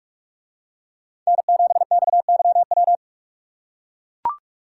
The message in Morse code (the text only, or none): N7XYW